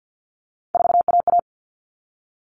Morse code, heard as a keyed tone, about 50 wpm, 700 Hz. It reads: 4AA
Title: All Cut Numbers - 3-Digit 50wpm